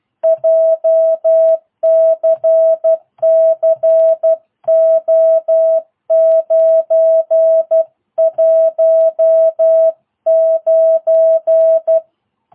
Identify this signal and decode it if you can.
Morse code, heard as a keyed tone, about 12 wpm, 650 Hz. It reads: JCC O919